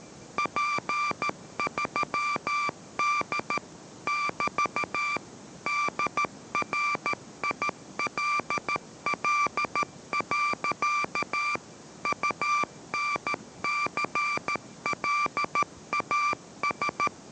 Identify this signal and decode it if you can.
Morse code, heard as a keyed tone, about 13 wpm, 1180 Hz, with noise. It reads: P3D <BT> DRILL. UNCLAS